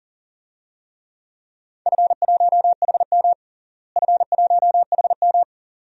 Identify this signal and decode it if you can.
Morse code, heard as a keyed tone, about 40 wpm, 700 Hz. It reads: F1HM F1HM
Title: DXCC-100 Call Signs - 4 Characters - Repeated 2x - 40wpm